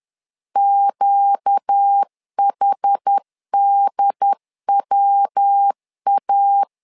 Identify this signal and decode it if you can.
Morse code, heard as a keyed tone, about 10 wpm, 800 Hz. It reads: QHDWA
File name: LEZIONI / 4 Q1 10wpm 2448